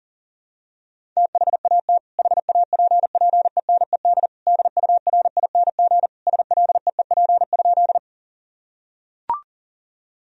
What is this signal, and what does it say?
Morse code, heard as a keyed tone, about 40 wpm, 700 Hz. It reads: THAT HAPPENED DURING SLEEP?
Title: Sentences from Top 1000 Words - Review Part 2 - 40wpm